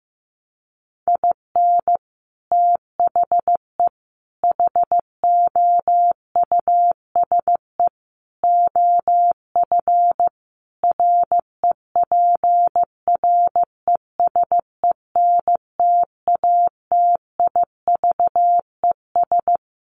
Morse code, heard as a keyed tone, about 15 wpm, 700 Hz. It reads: IN THE HOUSE OF REPRESENTATIVES
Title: Sets of 5 Words Encore 15wpm